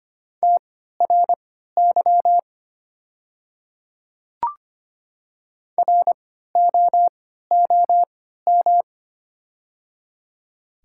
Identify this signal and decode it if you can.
Morse code, heard as a keyed tone, about 25 wpm, spaced as for 14 wpm, 700 Hz. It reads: TRY ROOM
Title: 3x Character Spacing - Top 500 Words - 25wpm